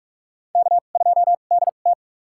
Morse code, heard as a keyed tone, about 45 wpm, spaced as for 32 wpm, 700 Hz. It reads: K2DT
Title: USA Call Signs - 2x Character Spacing - 45wpm